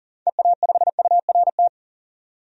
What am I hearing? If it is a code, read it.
Morse code, heard as a keyed tone, about 40 wpm, 700 Hz. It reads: EA5URT
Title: DXCC-100 Call Signs - 6 Characters - 40wpm